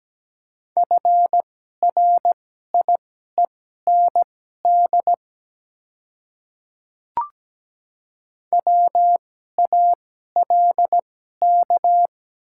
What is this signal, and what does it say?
Morse code, heard as a keyed tone, about 17 wpm, spaced as for 12 wpm, 700 Hz. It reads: FRIEND WALK
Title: 2x Character Spacing - Top 300 Words - 17wpm